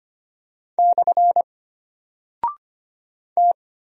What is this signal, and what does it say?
Morse code, heard as a keyed tone, about 25 wpm, 700 Hz. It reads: / T
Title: Single Character (Letter:Number:Punctuation:Prosign) Alt 25wpm